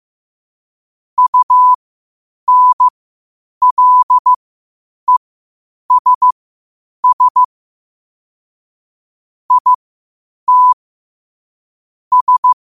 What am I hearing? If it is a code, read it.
Morse code, heard as a keyed tone, about 15 wpm, spaced as for 8 wpm, 1000 Hz. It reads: UNLESS IT S
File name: qotdfiles / 8wpm